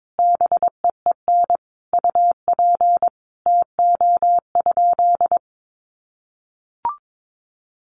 Morse code, heard as a keyed tone, about 22 wpm, 700 Hz. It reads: BEEN UP TO?
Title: Sentences from Top 100 Words 22wpm